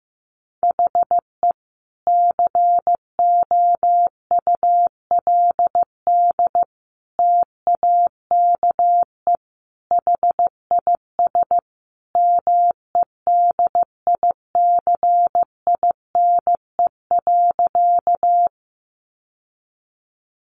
Morse code, heard as a keyed tone, about 15 wpm, 700 Hz. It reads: HE COULD TAKE HIS MEDICINE.
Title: Sentences from Top 900 Words 15wpm